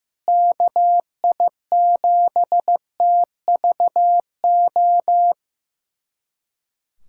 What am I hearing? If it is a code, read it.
Morse code, heard as a keyed tone, about 15 wpm, 700 Hz. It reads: KI7TVO
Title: USA Callsign DE Callsign - 6 Characters - 15wpm